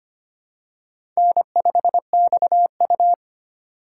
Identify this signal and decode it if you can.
Morse code, heard as a keyed tone, about 25 wpm, 700 Hz. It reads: N5XU
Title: USA Call Signs - 4 Characters - 2x Character Spacing - 25wpm